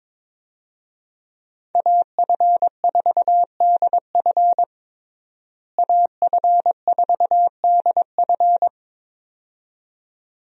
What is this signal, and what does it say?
Morse code, heard as a keyed tone, about 22 wpm, 700 Hz. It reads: AF4DF AF4DF
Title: USA Call Signs - 5 Characters - Repeated 2x - 22wpm